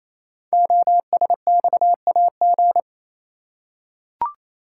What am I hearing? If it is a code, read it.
Morse code, heard as a keyed tone, about 29 wpm, 700 Hz. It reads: OSXAG
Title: Code Groups - 5 Letters - 28wpm